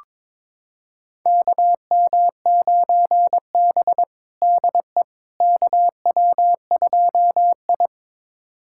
Morse code, heard as a keyed tone, about 22 wpm, 700 Hz. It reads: KM9B DE KW2I
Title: USA Callsign DE Callsign - 4 Characters - 22wpm